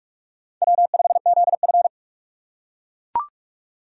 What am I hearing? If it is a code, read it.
Morse code, heard as a keyed tone, about 45 wpm, 700 Hz. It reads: W5ZF